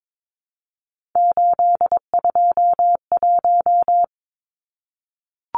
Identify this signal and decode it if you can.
Morse code, heard as a keyed tone, about 22 wpm, 700 Hz. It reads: 821
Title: Numbers - 3-Digit 22wpm